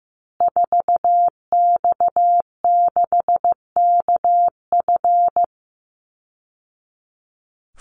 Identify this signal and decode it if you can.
Morse code, heard as a keyed tone, about 15 wpm, 700 Hz. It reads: 4X6KF
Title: Call Signs Repeated 2x - 15wpm